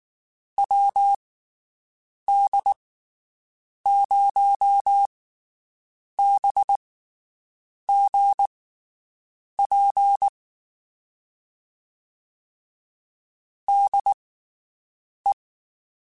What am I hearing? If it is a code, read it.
Morse code, heard as a keyed tone, about 19 wpm, spaced as for 7 wpm, 800 Hz. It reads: WD0BGP DE
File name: (spaced) / QSOs Spaced-mp3s / WD0BGP 1 @18 sp